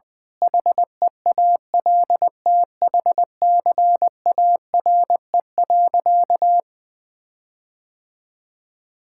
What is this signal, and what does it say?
Morse code, heard as a keyed tone, about 20 wpm, 700 Hz. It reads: HEALTHCARE.
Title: Sentences with 7-Word Phrases - Spoken Beginning - 20wpm